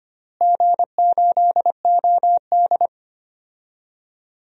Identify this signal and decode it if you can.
Morse code, heard as a keyed tone, about 25 wpm, 700 Hz. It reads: G8OD
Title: DXCC-100 Call Signs - 4 Characters - Repeated 3x - 25wpm